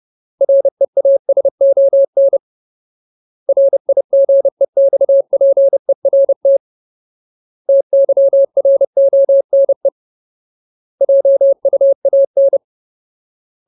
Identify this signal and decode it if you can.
Morse code, heard as a keyed tone, about 30 wpm, 550 Hz. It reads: REASON RIGEXPERT TYRONE JUAN
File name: Ultimate Word List / 30WPM Ultimate